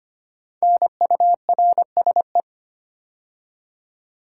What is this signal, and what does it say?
Morse code, heard as a keyed tone, about 25 wpm, 700 Hz. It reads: NURSE
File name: 5 Letter Words - First 3 Letters Given - 25wpm